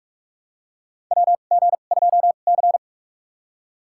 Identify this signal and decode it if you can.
Morse code, heard as a keyed tone, about 43 wpm, spaced as for 32 wpm, 700 Hz. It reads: WG2C
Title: USA Call Signs - 4 Characters - 2x Character Spacing - 45wpm